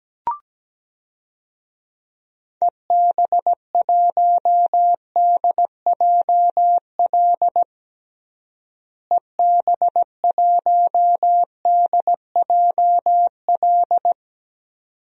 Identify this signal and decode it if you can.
Morse code, heard as a keyed tone, about 17 wpm, 700 Hz. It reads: EB1DJL EB1DJL
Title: DXCC-100 Call Signs - 6 Characters - Repeated 2x - 17wpm